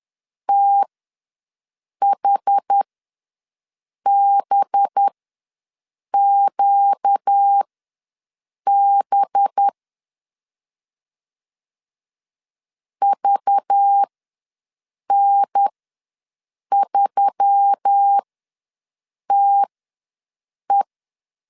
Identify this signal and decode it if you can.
Morse code, heard as a keyed tone, about 10 wpm, spaced as for 6 wpm, 800 Hz. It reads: THBQB VN3TE
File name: LEZIONI / 5 LBVC / 5 LBV ws 10wpm 3645